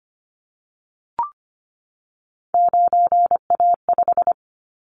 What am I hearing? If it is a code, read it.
Morse code, heard as a keyed tone, about 25 wpm, 700 Hz. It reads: 9A5
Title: Callsign Prefixes for Top 100 DXCC - 25wpm